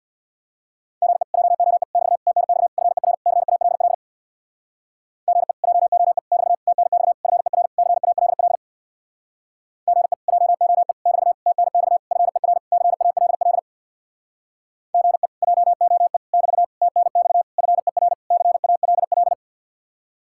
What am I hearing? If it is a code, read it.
Morse code, heard as a keyed tone, about 64 wpm, 700 Hz. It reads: GE JOE <BT> TNX FER CALL GE JOE <BT> TNX FER CALL GE JOE <BT> TNX FER CALL GE JOE <BT> TNX FER CALL